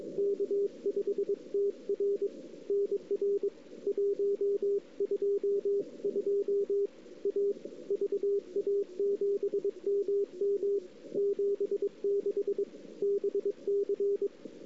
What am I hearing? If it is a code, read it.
Morse code, heard as a keyed tone, about 26 wpm, 410 Hz. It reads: K5TR NR 122 A VA7MM 76 BC
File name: CW / 2008-ss-cw-k5tr-op-n5rz -03:00